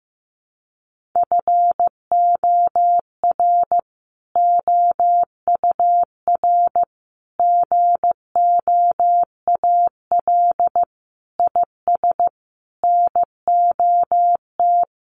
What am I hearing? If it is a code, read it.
Morse code, heard as a keyed tone, about 15 wpm, 700 Hz. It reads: FOR OUR GOAL IS NOT
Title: Sentences from Top 1400 Words 15wpm